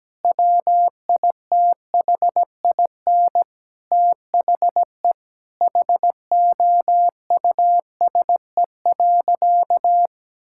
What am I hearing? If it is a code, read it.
Morse code, heard as a keyed tone, about 17 wpm, 700 Hz. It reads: WITHIN THE HOUSE.
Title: Sentences from Top 1000 Words - Review Part 6 - 17wpm